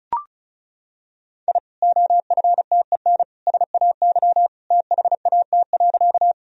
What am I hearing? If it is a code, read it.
Morse code, heard as a keyed tone, about 34 wpm, 700 Hz. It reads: I OFTEN SAY THAT.
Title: Sentences from Top 1000 Words - Review Part 1 - 35wpm